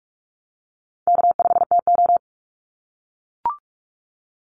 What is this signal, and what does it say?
Morse code, heard as a keyed tone, about 44 wpm, 700 Hz. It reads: K5TO
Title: Easy Call Signs - 45wpm